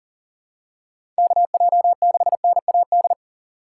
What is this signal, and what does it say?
Morse code, heard as a keyed tone, about 40 wpm, 700 Hz. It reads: KJ6NAD
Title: USA Call Signs - 6 Characters - 4x Character Spacing - 40wpm